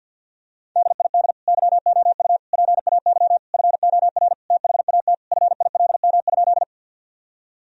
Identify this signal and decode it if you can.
Morse code, heard as a keyed tone, about 50 wpm, 700 Hz. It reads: DID YOU PAY FOR THAT FILM?
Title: Sentences from Top 1000 Words - Review Part 3 - 50wpm